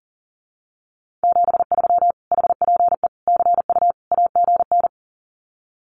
Morse code, heard as a keyed tone, about 40 wpm, 700 Hz. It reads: 73 HPE CU AGN
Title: Common QSO Phrases Encore 40wpm